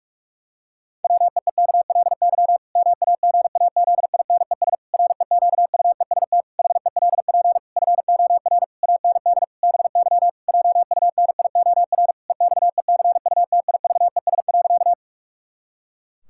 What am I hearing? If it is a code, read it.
Morse code, heard as a keyed tone, about 46 wpm, 700 Hz. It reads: WEEKLY MAGAZINES REQUEST HELP FOR AND BY JUNIOR EXECUTIVES.
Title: Pangram Challenge v2 - 45wpm